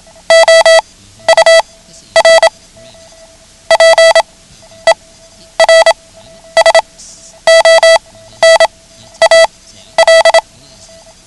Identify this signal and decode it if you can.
Morse code, heard as a keyed tone, about 27 wpm, spaced as for 12 wpm, 660 Hz. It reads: OUR PERSONAL
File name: ham radio / CodeQuick / Code Quick 09